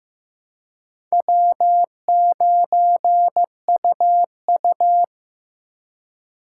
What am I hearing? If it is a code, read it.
Morse code, heard as a keyed tone, about 15 wpm, 700 Hz. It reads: W9UU
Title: USA Call Signs - 4 Characters - 2x Character Spacing - 15wpm